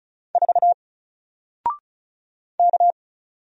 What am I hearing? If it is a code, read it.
Morse code, heard as a keyed tone, about 35 wpm, 700 Hz. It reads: 4 K
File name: Single-Letter-Number-35wpm